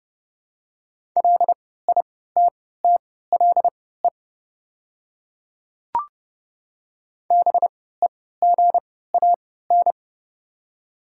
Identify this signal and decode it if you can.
Morse code, heard as a keyed tone, about 30 wpm, spaced as for 17 wpm, 700 Hz. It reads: LITTLE BEGAN
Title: 3x Character Spacing - Top 500 Words - 30wpm